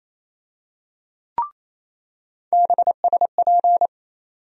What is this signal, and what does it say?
Morse code, heard as a keyed tone, about 28 wpm, 700 Hz. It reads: BSP
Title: USA Call Sign Postfixes - 3 Letters 28wpm